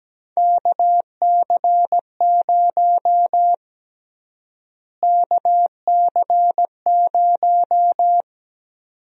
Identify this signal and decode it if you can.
Morse code, heard as a keyed tone, about 17 wpm, 700 Hz. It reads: KC0 KC0
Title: USA Call Sign Prefixes - Repeated 2x - 17wpm